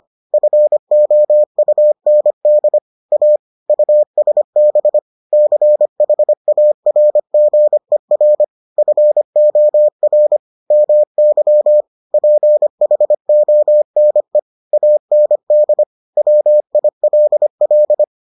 Morse code, heard as a keyed tone, about 25 wpm, 600 Hz. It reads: FOUND A USB CHARGER FOR MY PHONE AND WILL